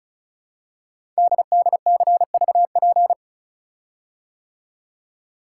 Morse code, heard as a keyed tone, about 36 wpm, 690 Hz. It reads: DDCVP
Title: Code Groups - 5 Letters - 35wpm